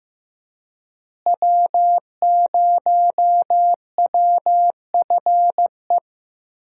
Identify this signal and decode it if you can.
Morse code, heard as a keyed tone, about 15 wpm, 700 Hz. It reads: W0WFE
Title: USA Call Signs - 5 Characters - Repeated 3x - 15wpm